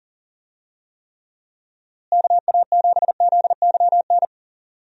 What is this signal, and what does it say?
Morse code, heard as a keyed tone, about 40 wpm, 700 Hz. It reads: KA7ZYN
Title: USA Call Signs - 6 Characters - Repeated 2x - 40wpm